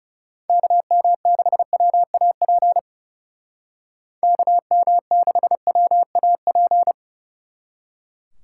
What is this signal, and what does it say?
Morse code, heard as a keyed tone, about 36 wpm, 700 Hz. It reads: KM6WAP KM6WAP
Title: Speed Racing - USA Call Signs 30wpm